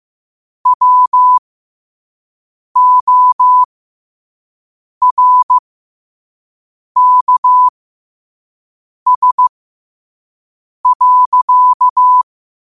Morse code, heard as a keyed tone, about 15 wpm, spaced as for 5 wpm, 1000 Hz. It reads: WORKS.
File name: qotdfiles / 5wpm